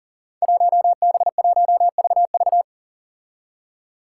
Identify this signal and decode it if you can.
Morse code, heard as a keyed tone, about 39 wpm, 700 Hz. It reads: 1B1VV
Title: Five Challenging Letter-Number Combinations 40wpm